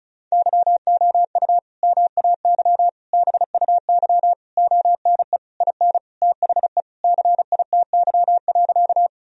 Morse code, heard as a keyed tone, about 36 wpm, 700 Hz. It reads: YOU MAY BUY ONE IN THE CITY.
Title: Sentences from Top 1000 Words - Review Part 4 - 35wpm